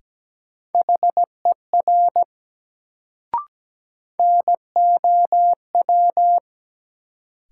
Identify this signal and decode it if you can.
Morse code, heard as a keyed tone, about 17 wpm, 700 Hz. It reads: HER NOW
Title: Sentences from Top 100 Words - New Word Drill - 17wpm